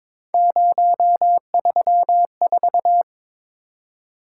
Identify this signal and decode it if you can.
Morse code, heard as a keyed tone, about 23 wpm, 700 Hz. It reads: 034
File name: practice / 22wpm / Three Letter-Number Combinations - 22wpm